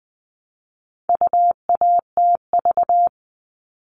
Morse code, heard as a keyed tone, about 20 wpm, 700 Hz. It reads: UATV
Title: All Cut Numbers - 4-Digit 20wpm